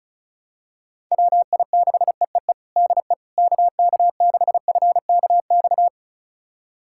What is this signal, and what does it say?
Morse code, heard as a keyed tone, about 36 wpm, 700 Hz. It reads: WI6EEE DE KK6FKX
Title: USA Callsign DE Callsign - 6 Characters - 35wpm